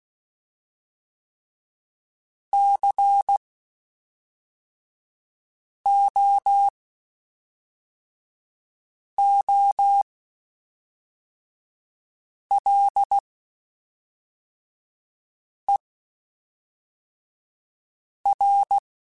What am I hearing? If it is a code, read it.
Morse code, heard as a keyed tone, about 16 wpm, spaced as for 3 wpm, 800 Hz. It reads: COOLER
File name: Spaced Characters / Spaced Characters-mp3s / Weather5 5 sp @5